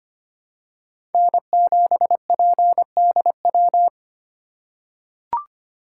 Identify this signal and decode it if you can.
Morse code, heard as a keyed tone, about 25 wpm, 700 Hz. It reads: N7PDW
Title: USA Call Signs - 5 Characters - 2x Character Spacing - 25wpm